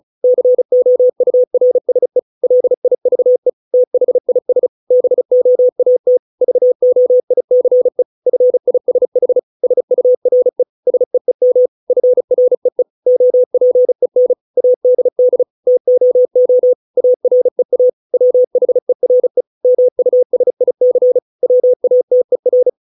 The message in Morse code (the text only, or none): COURSE LIVE THIS BOAT VOICE FISH SURE SEEM FREE OPEN ADD TOO AREA WHERE MUSIC WATER